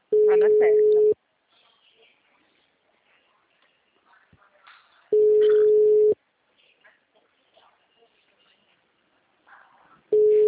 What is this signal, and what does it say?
Ringback tone (Europe, 425 Hz), with speech.
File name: out- -2004- - - .70143